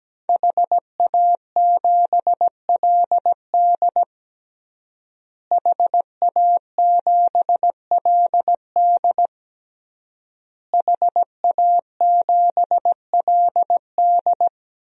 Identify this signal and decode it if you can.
Morse code, heard as a keyed tone, about 17 wpm, 700 Hz. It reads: HA7LD HA7LD HA7LD
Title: DXCC-100 Call Signs - 5 Characters - Repeated 3x - 17wpm